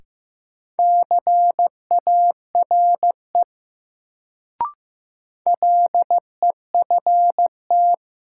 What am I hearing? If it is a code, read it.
Morse code, heard as a keyed tone, about 15 wpm, 700 Hz. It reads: CARE LEFT
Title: Sentences from Top 300 Words - New Word Drill - 15wpm